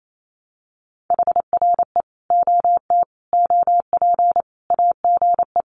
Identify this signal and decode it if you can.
Morse code, heard as a keyed tone, about 28 wpm, 700 Hz. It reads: HRE OT OP AGE
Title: Common QSO Phrases 28wpm